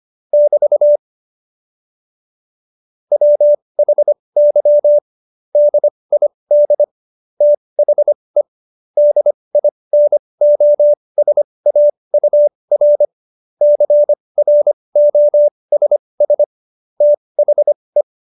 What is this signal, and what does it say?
Morse code, heard as a keyed tone, about 25 wpm, spaced as for 20 wpm, 590 Hz. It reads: <BT> WHY DID THE DINOSAUR CROSS THE
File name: Jokes-25-char-20-eff